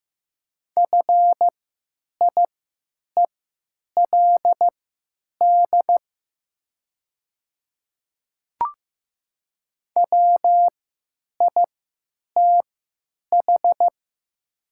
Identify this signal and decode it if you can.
Morse code, heard as a keyed tone, about 15 wpm, spaced as for 9 wpm, 700 Hz. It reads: FIELD WITH
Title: 3x Character Spacing - Top 500 Words - 15wpm